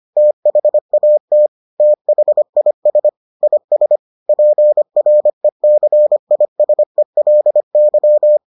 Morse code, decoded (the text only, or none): THAT THIS IS PRECISELY